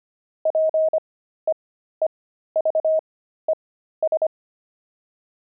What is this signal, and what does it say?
Morse code, heard as a keyed tone, about 25 wpm, spaced as for 13 wpm, 630 Hz. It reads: PEEVES
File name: pangram-fs-13-cs-25-linoleum